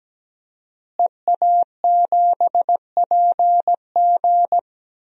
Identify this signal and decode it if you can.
Morse code, heard as a keyed tone, about 17 wpm, 700 Hz. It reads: EA7PG
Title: DXCC-100 Call Signs - 5 Characters - Repeated 3x - 17wpm